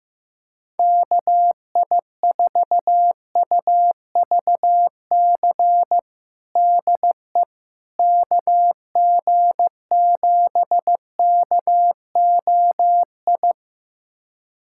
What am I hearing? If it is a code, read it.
Morse code, heard as a keyed tone, about 15 wpm, 700 Hz. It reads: KI4UVC DE KG7KOI
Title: USA Callsign DE Callsign - 6 Characters - 15wpm